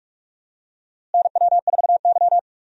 Morse code, heard as a keyed tone, about 42 wpm, 700 Hz. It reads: NW4Y
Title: USA Call Signs - 4 Characters - Repeated 2x - 45wpm